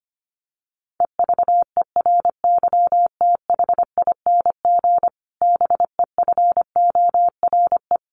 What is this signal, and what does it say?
Morse code, heard as a keyed tone, about 25 wpm, 700 Hz. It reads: EVERYTHING BEFORE
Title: Sentences from Top 1500 Words 25wpm